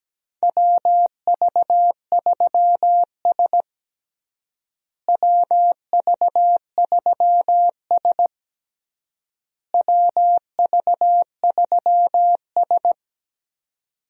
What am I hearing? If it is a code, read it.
Morse code, heard as a keyed tone, about 17 wpm, 700 Hz. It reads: WV3S WV3S WV3S
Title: USA Call Signs - 4 Characters - Repeated 3x - 17wpm